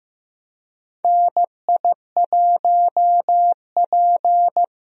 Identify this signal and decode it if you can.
Morse code, heard as a keyed tone, about 15 wpm, 700 Hz. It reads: NI1P
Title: USA Call Signs - 4 Characters - Repeated 3x - 15wpm